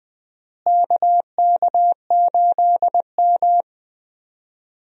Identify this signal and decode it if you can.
Morse code, heard as a keyed tone, about 20 wpm, 700 Hz. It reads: KK8M
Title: USA Call Signs - 4 Characters - Repeated 3x - 20wpm